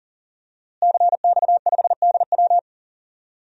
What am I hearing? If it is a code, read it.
Morse code, heard as a keyed tone, about 40 wpm, 700 Hz. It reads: CX5DW
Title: DXCC-100 Call Signs - 5 Characters - Repeated 3x - 40wpm